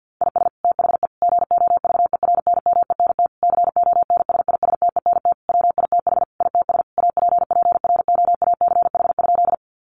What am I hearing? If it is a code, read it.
Morse code, heard as a keyed tone, about 50 wpm, 700 Hz. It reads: IS THE GOVERNMENT CONSISTENT WITH ITS APPROACH?
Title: Sentences from Top 1500 Words 50wpm